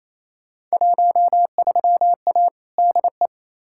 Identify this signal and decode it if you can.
Morse code, heard as a keyed tone, about 28 wpm, 700 Hz. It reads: 13A DE
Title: Field Day Exchanges Encore - 28wpm